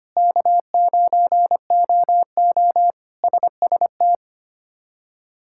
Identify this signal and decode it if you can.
Morse code, heard as a keyed tone, about 25 wpm, 700 Hz. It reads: K9OO SST
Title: K1USN SST QSO Practice - 25wpm